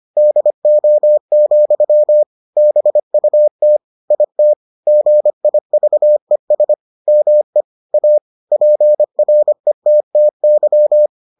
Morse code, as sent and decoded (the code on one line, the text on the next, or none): -.. --- --..-- / -... ..- - / .. - / --. .. ...- . ... / -- . / .- / .--. .-. - - -.--
DO, BUT IT GIVES ME A PRETTY